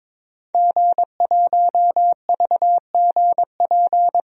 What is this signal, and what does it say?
Morse code, heard as a keyed tone, about 22 wpm, 700 Hz. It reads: G1VGP
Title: DXCC-100 Call Signs - 5 Characters - 22wpm